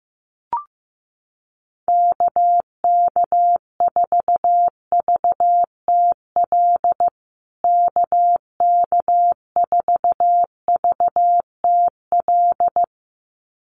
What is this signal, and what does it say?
Morse code, heard as a keyed tone, about 15 wpm, 700 Hz. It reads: KK4VTL KK4VTL
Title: USA Call Signs Repeated 2x 15wpm